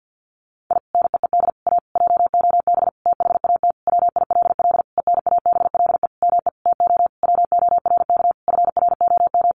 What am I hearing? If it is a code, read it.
Morse code, heard as a keyed tone, about 50 wpm, 700 Hz. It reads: I NEED A JOB THAT WILL ENABLE ME TO WORK FROM
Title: Sentences from Top 1500 Words 50wpm